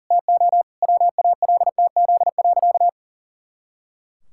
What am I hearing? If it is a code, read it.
Morse code, heard as a keyed tone, about 40 wpm, 700 Hz. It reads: TO WALTZ.
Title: Pangram Challenge v2 - 40wpm